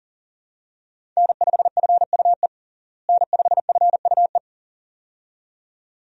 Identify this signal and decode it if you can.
Morse code, heard as a keyed tone, about 40 wpm, 700 Hz. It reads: N5FUE N5FUE
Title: USA Call Signs - 5 Characters - Repeated 2x - 40wpm